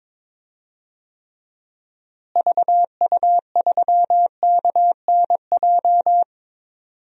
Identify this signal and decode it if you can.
Morse code, heard as a keyed tone, about 22 wpm, 700 Hz. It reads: VU3KNJ
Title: DXCC-100 Call Signs - 6 Characters - Repeated 2x - 22wpm